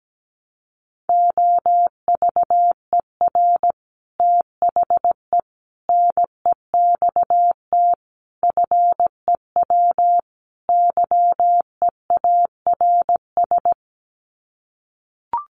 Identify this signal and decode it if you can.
Morse code, heard as a keyed tone, about 17 wpm, 700 Hz. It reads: OVER THE NEXT FEW YEARS
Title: Sets of 5 Words Encore 17wpm